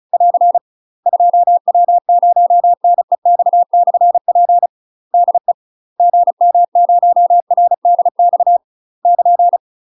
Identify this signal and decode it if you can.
Morse code, heard as a keyed tone, about 36 wpm, 700 Hz. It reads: <AR> 2W0NEX/P DE GM0RDX <KN>
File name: UK MASTER QSOs TRAINING / 35WPM / UK MASTER QSOs 0010